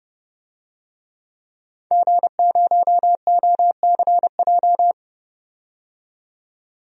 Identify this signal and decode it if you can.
Morse code, heard as a keyed tone, about 30 wpm, 700 Hz. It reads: G0OCJ
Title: DXCC-100 Call Signs - 5 Characters - 30wpm